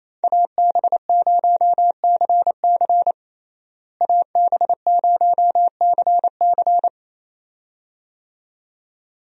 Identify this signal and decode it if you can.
Morse code, heard as a keyed tone, about 28 wpm, 700 Hz. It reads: AB0CC AB0CC
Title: USA Call Signs - 5 Characters - Repeated 2x - 28wpm